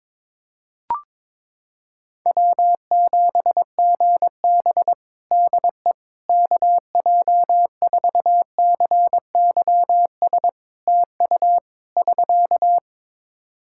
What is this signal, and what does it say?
Morse code, heard as a keyed tone, about 22 wpm, 700 Hz. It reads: W7GB DE KJ4CYS TU <SK>
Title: Practice QSO - Exchange 4 - Ending - 22wpm